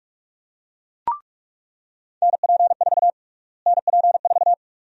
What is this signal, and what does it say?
Morse code, heard as a keyed tone, about 41 wpm, 700 Hz. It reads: NP4 NP4
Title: USA Call Sign Prefixes - Repeated 2x - 45wpm